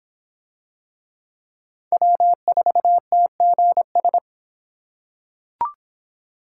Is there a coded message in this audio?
Yes: Morse code from 1.9 to 4.2 s, heard as a keyed tone, about 26 wpm, 700 Hz. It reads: W4TGS